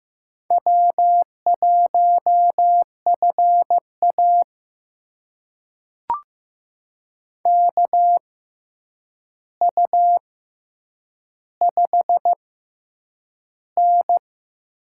Morse code, heard as a keyed tone, about 15 wpm, spaced as for 8 wpm, 700 Hz. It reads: W1FA KU5N
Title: USA Call Signs - 4 Characters - 6x Character Spacing - 15wpm